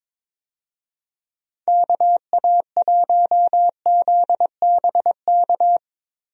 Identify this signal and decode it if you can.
Morse code, heard as a keyed tone, about 22 wpm, 700 Hz. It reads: KA1ZBK
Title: USA Call Signs - 6 Characters - Repeated 3x - 22wpm